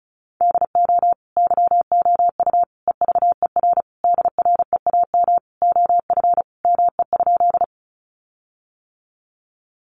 Morse code, heard as a keyed tone, about 34 wpm, 700 Hz. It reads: DO YOU EVER DREAM OF ME?
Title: Sentences from Top 1500 Words 35wpm